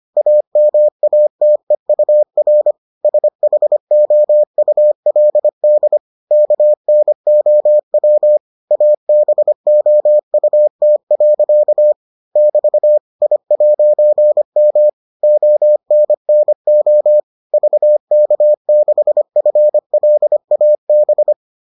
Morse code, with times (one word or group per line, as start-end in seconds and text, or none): 0.2-2.7 AMATEUR
3.0-6.0 SHOULD
6.3-8.4 KNOW
8.7-11.9 ABOUT.
12.4-13.0 <BT>
13.2-14.9 I'M
15.2-17.2 ONNO
17.5-21.3 VK6FLAB